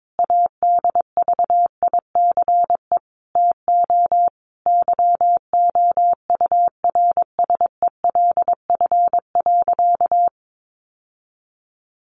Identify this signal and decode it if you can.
Morse code, heard as a keyed tone, about 22 wpm, 700 Hz. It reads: ADVICE TO YOURSELF.
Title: Sentences from Top 1100 Words 22wpm